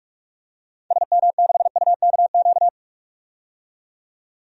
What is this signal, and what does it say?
Morse code, heard as a keyed tone, about 46 wpm, 700 Hz. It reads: SM6UKQ
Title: DXCC-100 Call Signs - 6 Characters - 45wpm